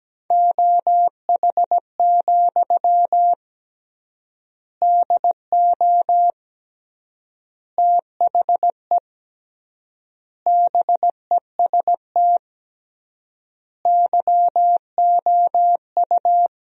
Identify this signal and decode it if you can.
Morse code, heard as a keyed tone, about 17 wpm, 700 Hz. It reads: OH, DO THE BEST YOU
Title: 3x Word-Spacing - Sentences from Top 500 Words - 17wpm